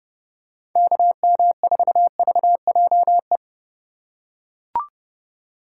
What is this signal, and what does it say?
Morse code, heard as a keyed tone, about 31 wpm, 700 Hz. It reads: KM4VJE
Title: USA Call Signs - 6 Characters - 4x Character Spacing - 30wpm